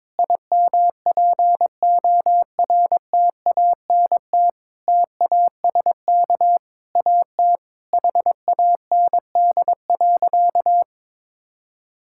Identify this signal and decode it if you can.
Morse code, heard as a keyed tone, about 22 wpm, 700 Hz. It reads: IMPORTANT TASK AT HAND.
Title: Sentences from Top 1600 Words 22wpm